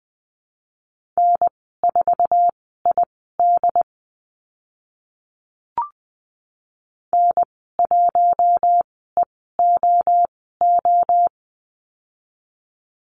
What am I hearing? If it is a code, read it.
Morse code, heard as a keyed tone, about 20 wpm, spaced as for 14 wpm, 700 Hz. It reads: N4ID N1EOO
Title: 069 Lesson 12g - Letter d - Call Signs 20wpm